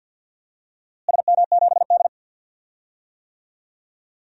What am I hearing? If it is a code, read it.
Morse code, heard as a keyed tone, about 53 wpm, 700 Hz. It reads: SM7D
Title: DXCC-100 Call Signs - 4 Characters - Repeated 2x - 50wpm